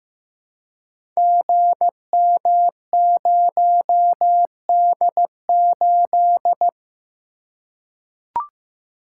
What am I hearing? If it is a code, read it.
Morse code, heard as a keyed tone, about 15 wpm, 700 Hz. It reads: GM0D8
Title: Code Groups - 5 Letter:Numbers - 15wpm